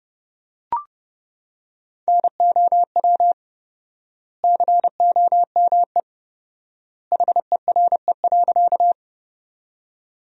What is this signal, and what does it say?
Morse code, heard as a keyed tone, about 30 wpm, 700 Hz. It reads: NOW COME HERE.
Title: 4x Word-Spacing - Sentences from Top 200 Words - 30wpm